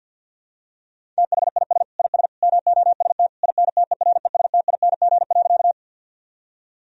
Morse code, heard as a keyed tone, about 46 wpm, 700 Hz. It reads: THIS IS MOST INTERESTING.
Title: Sentences from Top 1000 Words - Review Part 1 - 50wpm